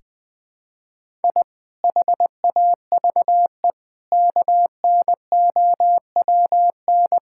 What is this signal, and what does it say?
Morse code, heard as a keyed tone, about 20 wpm, 700 Hz. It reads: I HAVE KNOWN
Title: Sentences with 4-Word Phrases - 20wpm